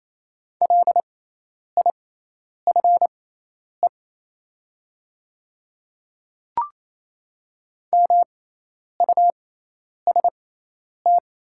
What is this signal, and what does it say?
Morse code, heard as a keyed tone, about 28 wpm, spaced as for 10 wpm, 700 Hz. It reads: LIFE MUST
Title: 6x Character Spacing - Top 300 Words - 28wpm